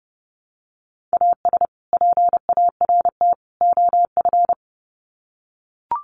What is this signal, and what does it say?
Morse code, heard as a keyed tone, about 30 wpm, 700 Hz. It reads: AS PART OF
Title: Sets of 3 Words 30wpm